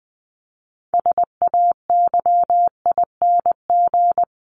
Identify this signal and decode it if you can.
Morse code, heard as a keyed tone, about 20 wpm, 700 Hz. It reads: SAYING
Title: ING Suffix - Top 100 Words with Base Words 20wpm